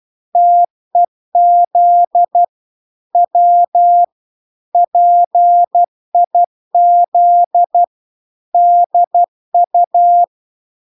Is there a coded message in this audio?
Yes: Morse code from 0.4 to 10.2 s, heard as a keyed tone, about 12 wpm, 700 Hz. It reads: TEZ W PIZDU